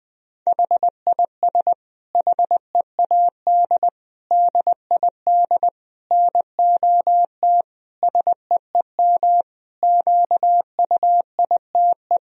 Morse code, heard as a keyed tone, about 20 wpm, 700 Hz. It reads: HIS HEAD DID NOT SEEM QUITE
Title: Sentences from Top 1000 Words - Review Part 2 - 20wpm